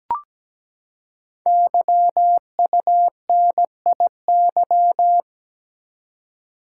Morse code, heard as a keyed tone, about 17 wpm, 700 Hz. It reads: YUNIY
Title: Code Groups - 5 Letters - 17wpm